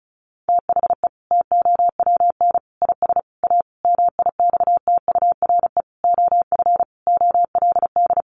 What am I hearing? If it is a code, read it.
Morse code, heard as a keyed tone, about 34 wpm, 700 Hz. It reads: THE TOWN IS A MIXTURE OF OLD
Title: Sentences from Top 1500 Words 35wpm